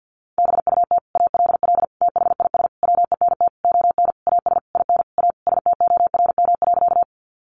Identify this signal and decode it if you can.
Morse code, heard as a keyed tone, about 50 wpm, 700 Hz. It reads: BUT ALL THIS WENT ON AS IN A STORM.
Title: Sentences from Top 1500 Words 50wpm